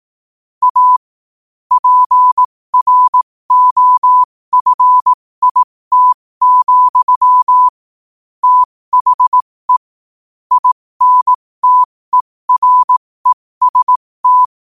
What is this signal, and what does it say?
Morse code, heard as a keyed tone, about 18 wpm, spaced as for 15 wpm, 1000 Hz. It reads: A PROFIT, THE INTEREST
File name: qotdfiles / 15wpm